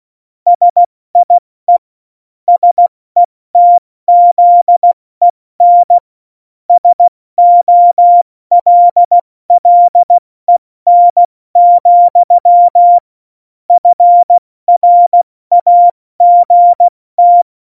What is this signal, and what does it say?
Morse code, heard as a keyed tone, about 16 wpm, 700 Hz. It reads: SIE SETZEN SOLLEN, FRAGT
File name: Morsekurs / Witze / Witz 18 August 2020, 70 F80